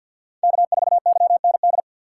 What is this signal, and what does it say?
Morse code, heard as a keyed tone, about 52 wpm, 700 Hz. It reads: K4YND
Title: K1USN SST QSO Practice - 50wpm